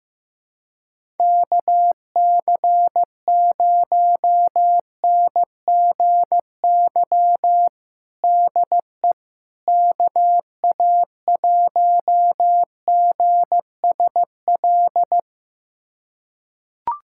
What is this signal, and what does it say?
Morse code, heard as a keyed tone, about 15 wpm, 700 Hz. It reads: KC0NGY DE KA1GSL
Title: USA Callsign DE Callsign - 6 Characters - 15wpm